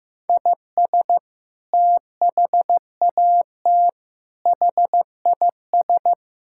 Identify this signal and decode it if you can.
Morse code, heard as a keyed tone, about 16 wpm, 700 Hz. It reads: IS THAT HIS
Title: Sentences from Top 1000 Words - Review Part 5 - 15wpm